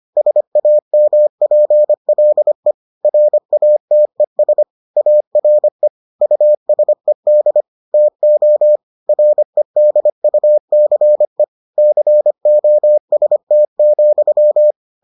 Morse code, heard as a keyed tone, about 25 wpm, 600 Hz. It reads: SAMPLE RATES ARE USED TO REDUCE COST,